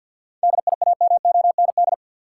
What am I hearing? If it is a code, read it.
Morse code, heard as a keyed tone, about 51 wpm, 700 Hz. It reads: DIAMOND
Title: Seven Letter Words - 50wpm